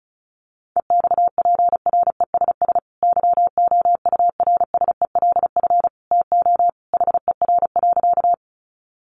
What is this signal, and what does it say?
Morse code, heard as a keyed tone, about 34 wpm, 700 Hz. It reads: EXPRESS YOURSELF TO HER.
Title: Sentences from Top 1100 Words 35wpm